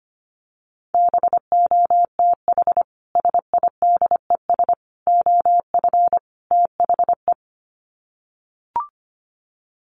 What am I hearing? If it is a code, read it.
Morse code, heard as a keyed tone, about 25 wpm, 700 Hz. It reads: BOTH SIDES OF THE
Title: Sets of 4 Words 25wpm